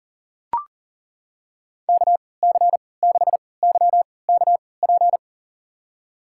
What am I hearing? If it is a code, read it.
Morse code, heard as a keyed tone, about 40 wpm, spaced as for 23 wpm, 700 Hz. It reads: KC6YKP
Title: USA Call Signs - 6 Characters - 3x Character Spacing - 40wpm